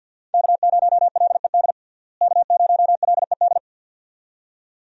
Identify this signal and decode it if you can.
Morse code, heard as a keyed tone, about 52 wpm, 700 Hz. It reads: K0LED K0LED
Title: USA Call Signs - 5 Characters - Repeated 2x - 50wpm